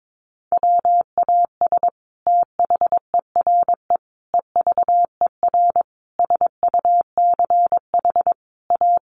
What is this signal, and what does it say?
Morse code, heard as a keyed tone, about 22 wpm, 700 Hz. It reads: WAS THERE EVER SUCH A
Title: Sentences from Top 400 Words 22wpm